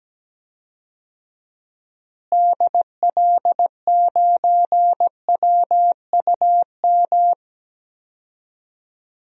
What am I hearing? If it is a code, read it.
Morse code, heard as a keyed tone, about 17 wpm, 700 Hz. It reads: DL9WUM
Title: DXCC-100 Call Signs - 6 Characters - Repeated 2x - 17wpm